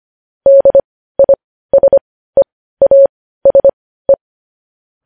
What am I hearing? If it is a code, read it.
Morse code, heard as a keyed tone, about 25 wpm, spaced as for 15 wpm, 550 Hz. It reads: DISEASE
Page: news-headlines-in-morse-code-at-15-wpm